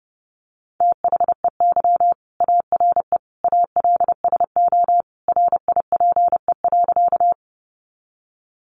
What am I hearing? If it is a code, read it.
Morse code, heard as a keyed tone, about 30 wpm, 700 Hz. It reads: THEY ARE ALSO RIPE.
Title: Additional Easy Phrases 30wpm